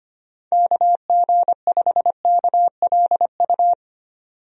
Morse code, heard as a keyed tone, about 25 wpm, 700 Hz. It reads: KG5KLU
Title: USA Call Signs - 6 Characters - Repeated 2x - 25wpm